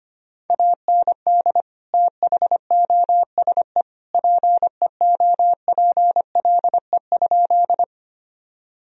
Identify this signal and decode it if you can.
Morse code, heard as a keyed tone, about 25 wpm, 700 Hz. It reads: AND THOSE PEOPLE?
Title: Sentences from Top 1000 Words - Review Part 1 - 25wpm